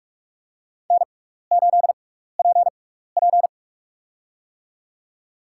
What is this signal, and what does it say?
Morse code, heard as a keyed tone, about 46 wpm, spaced as for 16 wpm, 700 Hz. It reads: N8PP
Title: USA Call Signs - 4 Characters - 6x Character Spacing - 45wpm